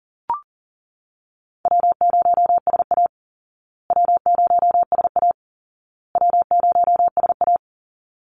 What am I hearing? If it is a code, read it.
Morse code, heard as a keyed tone, about 40 wpm, 700 Hz. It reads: W0SA W0SA W0SA
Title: Easy Call Signs Repeated 3x - 40wpm